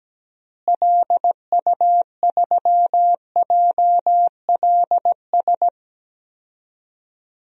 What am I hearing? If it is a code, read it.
Morse code, heard as a keyed tone, about 17 wpm, 700 Hz. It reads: LU3JLS
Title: DXCC-100 Call Signs - 6 Characters - Repeated 3x - 17wpm